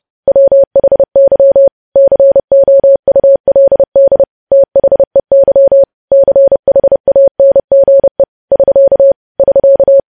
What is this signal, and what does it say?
Morse code, heard as a keyed tone, about 30 wpm, 550 Hz. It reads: WHY COULD THEY CHANGE <SK> <SK>
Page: news headlines in morse code at 30 wpm